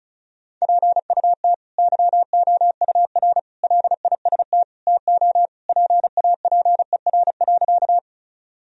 Morse code, heard as a keyed tone, about 36 wpm, 700 Hz. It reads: PUT YOUR LIST TO PAPER.
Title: Sentences from Top 1000 Words - Review Part 2 - 35wpm